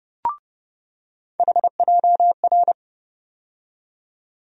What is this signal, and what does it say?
Morse code, heard as a keyed tone, about 31 wpm, 700 Hz. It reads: HJR
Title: USA Call Sign Postfixes - 3 Letters 30wpm